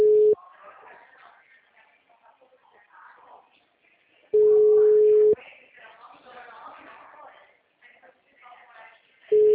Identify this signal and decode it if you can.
Ringback tone (Europe, 425 Hz), with speech.
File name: out- -2004- - - .105